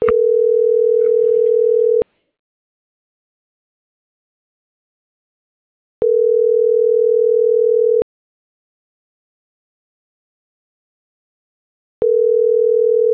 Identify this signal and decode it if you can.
Ringback tone (North America), 440 + 480 Hz.
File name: CENTRAL SAGOT / out danae.bsagot - .56794